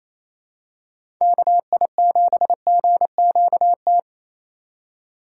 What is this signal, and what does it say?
Morse code, heard as a keyed tone, about 28 wpm, 700 Hz. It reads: KI7GQT
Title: USA Call Signs - 6 Characters - 2x Character Spacing - 28wpm